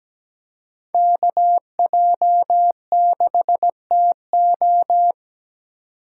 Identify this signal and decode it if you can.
Morse code, heard as a keyed tone, about 17 wpm, 700 Hz. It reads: KJ6TO
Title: USA Call Signs - 5 Characters - 3x Character Spacing - 17wpm